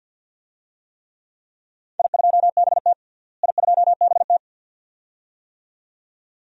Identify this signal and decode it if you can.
Morse code, heard as a keyed tone, about 50 wpm, 700 Hz. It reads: I2BT I2BT
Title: DXCC-100 Call Signs - 4 Characters - Repeated 2x - 50wpm